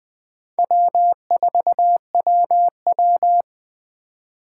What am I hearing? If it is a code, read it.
Morse code, heard as a keyed tone, about 21 wpm, 700 Hz. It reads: W4WW
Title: USA Call Signs - 3x Character Spacing - 20wpm